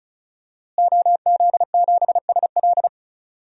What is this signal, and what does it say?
Morse code, heard as a keyed tone, about 36 wpm, 700 Hz. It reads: OZ7SL
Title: DXCC-100 Call Signs - 5 Characters - Repeated 2x - 35wpm